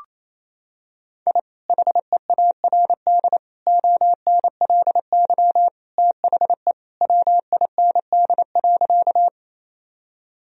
Morse code, heard as a keyed tone, about 28 wpm, 700 Hz. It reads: I HEARD ONLY THE WIND.
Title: Sentences from Top 1000 Words - Review Part 2 - 28wpm